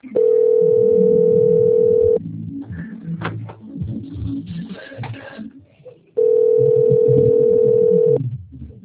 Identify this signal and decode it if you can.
Ringback tone (North America), 440 + 480 Hz.